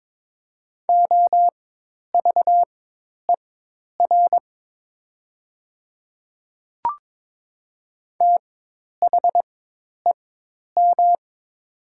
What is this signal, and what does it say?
Morse code, heard as a keyed tone, about 22 wpm, spaced as for 10 wpm, 700 Hz. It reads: OVER THEM
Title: 4x Character Spacing - Top 100 Words - 22wpm